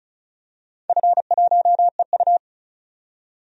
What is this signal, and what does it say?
Morse code, heard as a keyed tone, about 36 wpm, 700 Hz. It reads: F1EU
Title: DXCC-100 Call Signs - 4 Characters - 35wpm